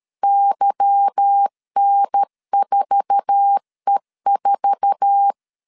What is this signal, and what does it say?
Morse code, heard as a keyed tone, about 13 wpm, 800 Hz. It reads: YN4E4
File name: CW / DET / 1 9 1500C S C 12wpm 3811